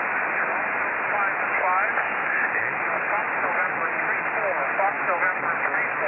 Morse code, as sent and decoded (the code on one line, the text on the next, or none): none